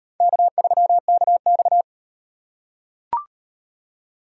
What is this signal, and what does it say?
Morse code, heard as a keyed tone, about 38 wpm, 700 Hz. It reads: K3KX